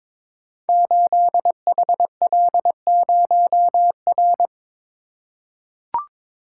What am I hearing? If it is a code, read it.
Morse code, heard as a keyed tone, about 23 wpm, 700 Hz. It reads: 8HL0R
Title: Code Groups - 5 Letter:Numbers - 22wpm